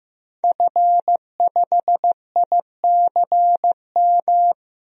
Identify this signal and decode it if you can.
Morse code, heard as a keyed tone, about 15 wpm, 700 Hz. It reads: F5ICM
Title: DXCC-100 Call Signs - 5 Characters - 15wpm